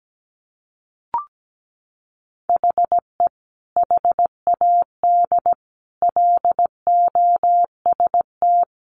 Morse code, heard as a keyed tone, about 17 wpm, 700 Hz. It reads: HE HAD LOST
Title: Sentences from Top 1400 Words 17wpm